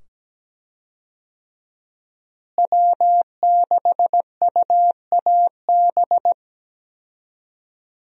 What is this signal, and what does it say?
Morse code, heard as a keyed tone, about 17 wpm, 700 Hz. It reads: W6UAB
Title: USA Call Signs - 5 Characters - Repeated 2x - 17wpm